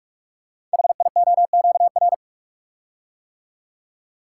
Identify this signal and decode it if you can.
Morse code, heard as a keyed tone, about 44 wpm, 690 Hz. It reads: HIOQR
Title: Code Groups - 5 Letter:Numbers - 45wpm